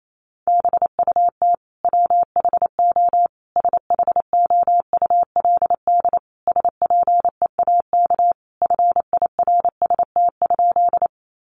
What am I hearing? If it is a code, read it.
Morse code, heard as a keyed tone, about 28 wpm, 700 Hz. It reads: BUT WHO SHOULD SPEAK FIRST?
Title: Sentences from Top 1100 Words 28wpm